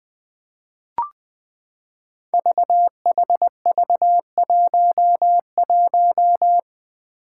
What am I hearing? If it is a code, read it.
Morse code, heard as a keyed tone, about 20 wpm, 700 Hz. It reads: VHV11
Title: Five Challenging Letter-Number Combinations 20wpm